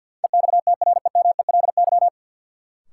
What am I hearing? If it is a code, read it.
Morse code, heard as a keyed tone, about 53 wpm, 700 Hz. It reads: EXTREMELY
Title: Sentences from Top 1000 Words - Review Part 7 - New Word Drill - 50wpm